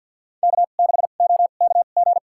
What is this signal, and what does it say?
Morse code, heard as a keyed tone, about 50 wpm, spaced as for 36 wpm, 690 Hz. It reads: K6OKG
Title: USA Call Signs - 2x Character Spacing - 50wpm